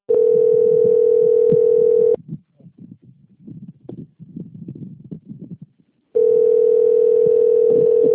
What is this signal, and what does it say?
Ringback tone (North America), 440 + 480 Hz.